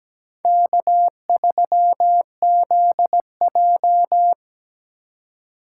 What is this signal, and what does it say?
Morse code, heard as a keyed tone, about 20 wpm, 700 Hz. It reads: K3ZJ
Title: Speed Racing - USA 1x2 Call Signs 17wpm